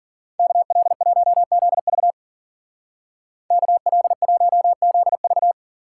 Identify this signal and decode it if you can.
Morse code, heard as a keyed tone, about 52 wpm, 700 Hz. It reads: KL1ZV KL1ZV